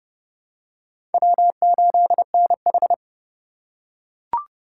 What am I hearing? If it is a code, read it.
Morse code, heard as a keyed tone, about 29 wpm, 700 Hz. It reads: W8NH
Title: USA Call Signs - 4 Characters - 6x Character Spacing - 30wpm